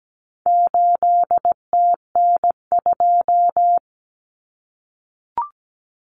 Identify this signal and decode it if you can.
Morse code, heard as a keyed tone, about 17 wpm, 700 Hz. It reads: 8TN2
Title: Common Cut Numbers - 4-Digit 17wpm